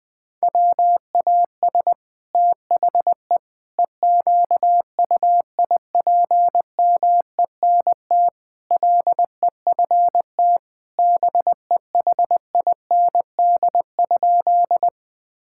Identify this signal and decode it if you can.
Morse code, heard as a keyed tone, about 21 wpm, 700 Hz. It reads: WAS THE EQUIPMENT LEFT BEHIND?
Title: Sentences from Top 1000 Words - Review Part 5 - 20wpm